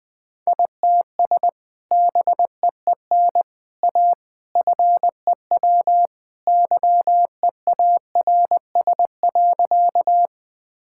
Morse code, heard as a keyed tone, about 20 wpm, 700 Hz. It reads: ITS BEEN A FEW YEARS.
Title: Sentences with 3-Word Phrases - 20wpm